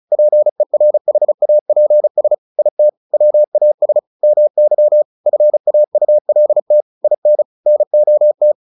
Morse code, heard as a keyed tone, about 34 wpm, 600 Hz. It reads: PERHAPS IT WAS MY FAULT IN NOT
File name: the old man and the sea 2 35wpm